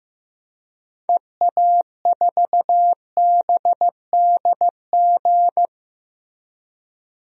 Morse code, heard as a keyed tone, about 15 wpm, 700 Hz. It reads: EA4BDG
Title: DXCC-100 Call Signs - 6 Characters - Repeated 3x - 15wpm